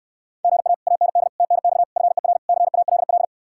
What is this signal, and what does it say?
Morse code, heard as a keyed tone, about 69 wpm, 700 Hz. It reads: GA AND TNX FER CALL